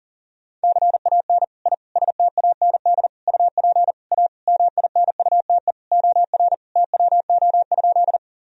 Morse code, heard as a keyed tone, about 40 wpm, 700 Hz. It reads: CAN I STAND UP A MINUTE OR TWO?
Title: Sentences from Top 1000 Words - Review Part 1 - 40wpm